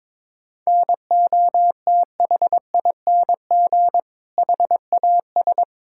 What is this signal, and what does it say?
Morse code, heard as a keyed tone, about 22 wpm, 700 Hz. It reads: NOTHING HAS
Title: Sentences from Top 1000 Words - Review Part 5 - 22wpm